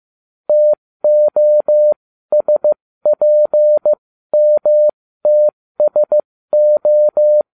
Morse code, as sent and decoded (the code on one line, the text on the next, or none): - --- ... .--. -- - ... ---
TOSPMTSO